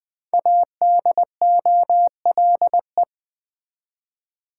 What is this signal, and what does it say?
Morse code, heard as a keyed tone, about 20 wpm, 700 Hz. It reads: ADOLE
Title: Build A Word - 5 Letters At A Time - 10-Letter Words - 20wpm